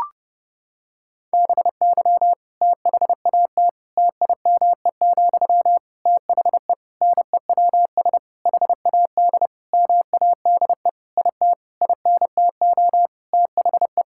Morse code, heard as a keyed tone, about 30 wpm, 700 Hz. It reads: BY THAT TIME, THE NEWS HAD MADE IT INTO THE
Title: Sentences from Top 1600 Words 30wpm